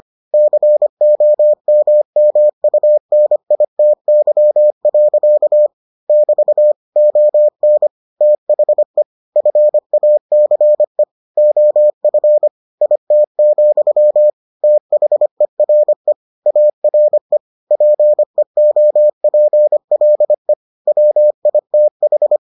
Morse code, heard as a keyed tone, about 25 wpm, 600 Hz. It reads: COMMUNITY. <BT> ON THE FACE OF IT, THERE ARE PEOPLE WITH